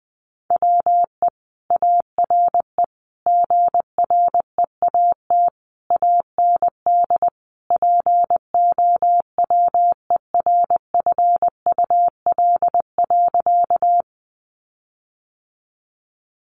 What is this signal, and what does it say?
Morse code, heard as a keyed tone, about 20 wpm, 700 Hz. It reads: WE ARE GREAT AND POWERFUL.
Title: Sentences from Top 1300 Words 20wpm